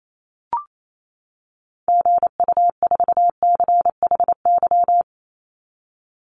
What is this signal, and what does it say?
Morse code, heard as a keyed tone, about 28 wpm, 700 Hz. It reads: GU4CHY
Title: Call Signs - 28wpm